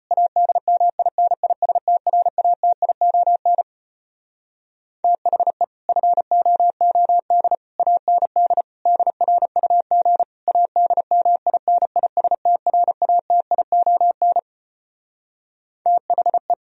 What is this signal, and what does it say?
Morse code, heard as a keyed tone, about 33 wpm, 700 Hz. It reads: ADMINISTRATION THE FOOD AND DRUG ADMINISTRATION THE